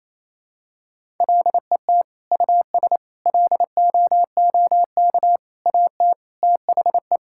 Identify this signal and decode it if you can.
Morse code, heard as a keyed tone, about 28 wpm, 700 Hz. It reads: LET US LOOK AT THE
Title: Sentences from Top 1000 Words - Review Part 5 - 28wpm